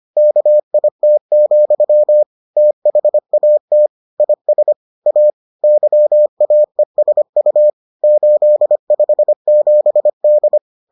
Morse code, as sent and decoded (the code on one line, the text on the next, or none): -.- .. - --..-- / - .... .- - / .. ... / .- / -.-- .- . ... ..- / ---.. ..... --... -..
KIT, THAT IS A YAESU 857D